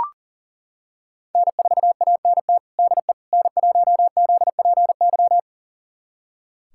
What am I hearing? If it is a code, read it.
Morse code, heard as a keyed tone, about 37 wpm, 700 Hz. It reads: N4ANT DE N1ZPY
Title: USA Callsign DE Callsign - 5 Characters - 40wpm